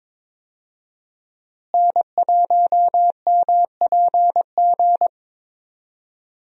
Morse code, heard as a keyed tone, about 21 wpm, 700 Hz. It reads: N1MPG
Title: USA Call Signs - 5 Characters - Repeated 2x - 22wpm